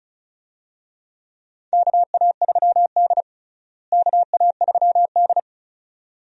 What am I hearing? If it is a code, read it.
Morse code, heard as a keyed tone, about 36 wpm, 700 Hz. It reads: KA3D KA3D
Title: USA Call Signs - 4 Characters - Repeated 2x - 35wpm